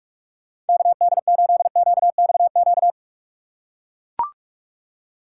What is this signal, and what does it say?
Morse code, heard as a keyed tone, about 41 wpm, 690 Hz. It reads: KD8QXQ
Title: USA Call Signs - 6 Characters - 2x Character Spacing - 45wpm